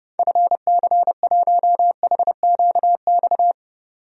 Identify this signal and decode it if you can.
Morse code, heard as a keyed tone, about 30 wpm, 700 Hz. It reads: FC1HQX
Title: DXCC-100 Call Signs - 6 Characters - 30wpm